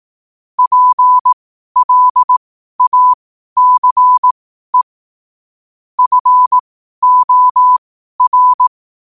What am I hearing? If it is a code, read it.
Morse code, heard as a keyed tone, about 18 wpm, spaced as for 12 wpm, 1000 Hz. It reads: PLACE FOR
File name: qotdfiles / 12wpm